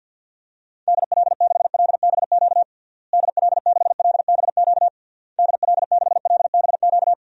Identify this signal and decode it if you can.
Morse code, heard as a keyed tone, about 53 wpm, 700 Hz. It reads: DL6LBQ DL6LBQ DL6LBQ
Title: DXCC-100 Call Signs - 6 Characters - Repeated 3x - 50wpm